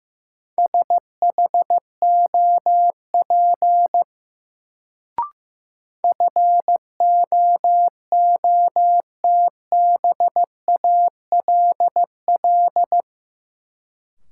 Morse code, heard as a keyed tone, about 15 wpm, 700 Hz. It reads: SHOP FOOTBALL
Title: Sentences from Top 1600 Words - New Word Drill - 15wpm